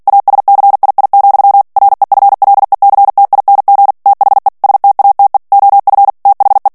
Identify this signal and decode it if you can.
Morse code, heard as a keyed tone, about 48 wpm, 800 Hz. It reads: ASCII, REFLECTING THE STATE OF THE
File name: ANGLAIS texte clair 3 vitesse 45 mots minute